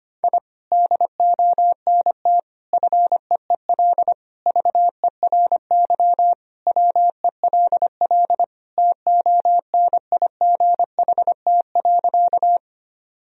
Morse code, heard as a keyed tone, about 25 wpm, 700 Hz. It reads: I DONT FEEL VERY WELL TONIGHT.
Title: Sentences from Top 1000 Words - Review Part 3 - 25wpm